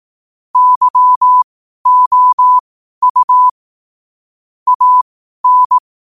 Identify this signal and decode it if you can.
Morse code, heard as a keyed tone, about 18 wpm, spaced as for 12 wpm, 1000 Hz. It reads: YOU AN
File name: qotdfiles / 12wpm